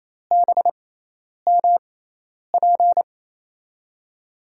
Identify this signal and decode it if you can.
Morse code, heard as a keyed tone, about 28 wpm, spaced as for 10 wpm, 700 Hz. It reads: BMP
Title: USA Call Signs - 6 Characters - 6x Character Spacing - 28wpm